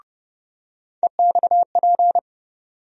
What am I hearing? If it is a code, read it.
Morse code, heard as a keyed tone, about 30 wpm, 700 Hz. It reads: EXP
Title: USA Call Sign Postfixes - 3 Letters 30wpm